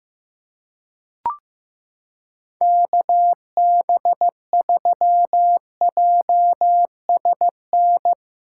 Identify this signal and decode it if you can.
Morse code, heard as a keyed tone, about 15 wpm, 700 Hz. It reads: KB3JSN
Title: USA Callsign DE Callsign - 6 Characters - 15wpm